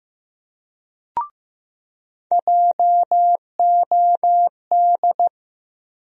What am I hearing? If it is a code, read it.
Morse code, heard as a keyed tone, about 15 wpm, 700 Hz. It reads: JOD
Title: USA Call Sign Postfixes - 3 Letters 15wpm